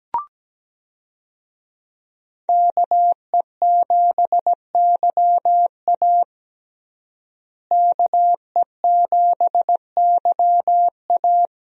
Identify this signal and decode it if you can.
Morse code, heard as a keyed tone, about 17 wpm, 700 Hz. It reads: KE7YA KE7YA
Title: USA Call Signs - 5 Characters - Repeated 3x - 17wpm